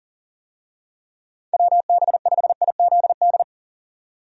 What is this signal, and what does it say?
Morse code, heard as a keyed tone, about 40 wpm, 700 Hz. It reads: WB5IZD
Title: USA Call Signs - 6 Characters - 40wpm